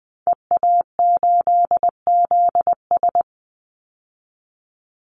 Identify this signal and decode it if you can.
Morse code, heard as a keyed tone, about 20 wpm, 700 Hz. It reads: EA8ZS
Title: Call Signs Repeated 3x - 20wpm